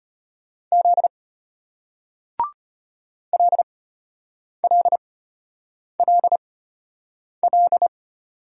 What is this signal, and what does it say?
Morse code, heard as a keyed tone, about 34 wpm, spaced as for 8 wpm, 700 Hz. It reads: Z LLLL